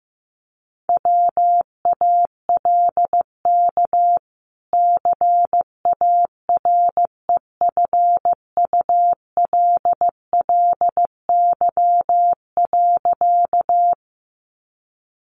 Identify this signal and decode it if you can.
Morse code, heard as a keyed tone, about 15 wpm, 700 Hz. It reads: WALK CAREFULLY.
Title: Sentences from Top 1300 Words 15wpm